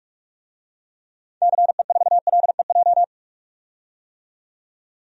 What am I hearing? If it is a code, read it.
Morse code, heard as a keyed tone, about 42 wpm, 700 Hz. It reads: CE4LEJ
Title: DXCC-100 Call Signs - 6 Characters - 45wpm